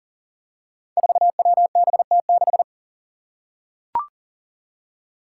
Morse code, heard as a keyed tone, about 40 wpm, 700 Hz. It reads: 4WBT6
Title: Five Challenging Letter-Number Combinations 40wpm